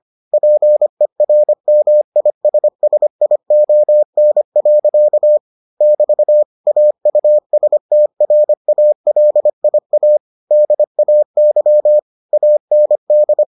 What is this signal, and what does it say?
Morse code, heard as a keyed tone, about 25 wpm, 600 Hz. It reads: PERMISSION. <BT> AUSTRALIA DAY AND